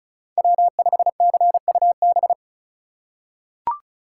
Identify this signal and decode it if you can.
Morse code, heard as a keyed tone, about 34 wpm, 700 Hz. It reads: W5CUB
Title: USA Call Signs - 5 Characters - 4x Character Spacing - 35wpm